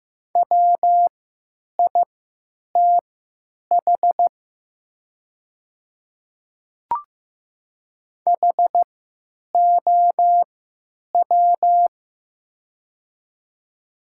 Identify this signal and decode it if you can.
Morse code, heard as a keyed tone, about 15 wpm, spaced as for 9 wpm, 700 Hz. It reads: WITH HOW
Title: 3x Character Spacing - Top 100 Words - 15wpm